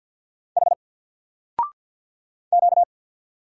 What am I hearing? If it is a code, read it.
Morse code, heard as a keyed tone, about 48 wpm, 700 Hz. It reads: H Q
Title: Single Character (Letter:Number:Punctuation:Prosign) 50wpm